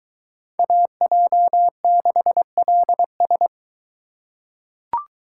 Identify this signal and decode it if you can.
Morse code, heard as a keyed tone, about 22 wpm, 700 Hz. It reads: AJ6LS